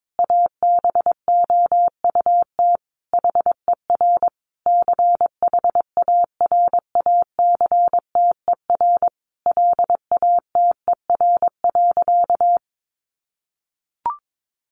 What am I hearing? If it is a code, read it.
Morse code, heard as a keyed tone, about 22 wpm, 700 Hz. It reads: ABOUT HER CHARACTER LATER.
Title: Sentences from Top 1200 Words 22wpm